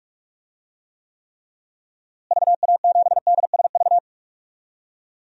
Morse code, heard as a keyed tone, about 43 wpm, 700 Hz. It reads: VA7DSV
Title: DXCC-100 Call Signs - 6 Characters - Repeated 3x - 45wpm